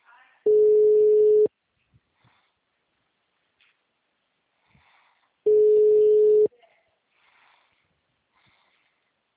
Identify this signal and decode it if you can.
Ringback tone (Europe, 425 Hz), with speech.
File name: out- -2006- - - .10